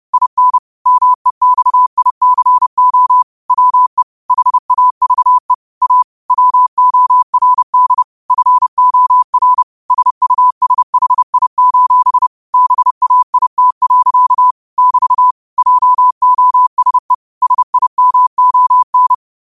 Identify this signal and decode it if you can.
Morse code, heard as a keyed tone, about 30 wpm, 1000 Hz. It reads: IN MEXICO WE HAVE A WORD FOR SUSHI: BAIT. <BT> JOSE SIMON
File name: qotdfiles / 30wpm